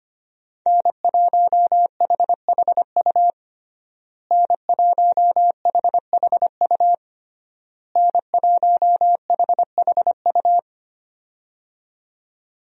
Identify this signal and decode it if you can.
Morse code, heard as a keyed tone, about 25 wpm, 700 Hz. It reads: N1HHU N1HHU N1HHU
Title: USA Call Signs - 5 Characters - Repeated 3x - 25wpm